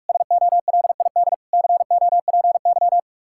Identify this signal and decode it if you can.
Morse code, heard as a keyed tone, about 44 wpm, 700 Hz. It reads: SOLID COPY